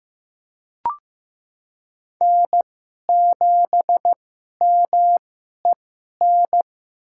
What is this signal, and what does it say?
Morse code, heard as a keyed tone, about 15 wpm, spaced as for 11 wpm, 700 Hz. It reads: N7MEN
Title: USA Call Signs - 5 Characters - 2x Character Spacing - 15wpm